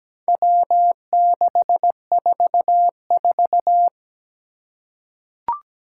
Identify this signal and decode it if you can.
Morse code, heard as a keyed tone, about 17 wpm, 700 Hz. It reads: W644
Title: Four Challenging Letter-Number Combinations 17wpm